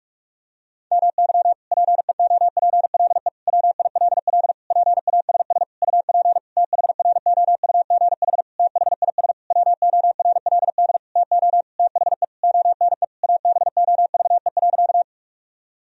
Morse code, heard as a keyed tone, about 41 wpm, 700 Hz. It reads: MY PEOPLE WILL PASS UP THROUGH THIS WORLD TO THE ONE ABOVE.
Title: Sentences from Top 1000 Words - Review Part 2 - 45wpm